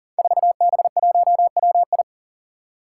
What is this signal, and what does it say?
Morse code, heard as a keyed tone, about 37 wpm, 700 Hz. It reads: 4B1WI
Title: Five Challenging Letter-Number Combinations 40wpm